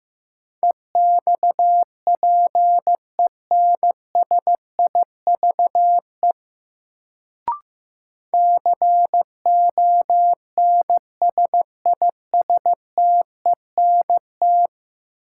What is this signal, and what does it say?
Morse code, heard as a keyed tone, about 15 wpm, 700 Hz. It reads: EXPENSIVE CONSISTENT
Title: Sentences from Top 1500 Words - New Word Drill Alt - 15wpm